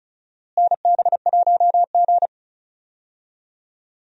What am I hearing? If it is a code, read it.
Morse code, heard as a keyed tone, about 36 wpm, 700 Hz. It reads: NB1G
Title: USA Call Signs - 4 Characters - Repeated 3x - 35wpm